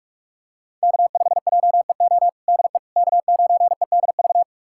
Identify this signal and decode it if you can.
Morse code, heard as a keyed tone, about 47 wpm, 700 Hz. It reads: K5JEO DE K9EDV
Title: USA Callsign DE Callsign - 5 Characters - 45wpm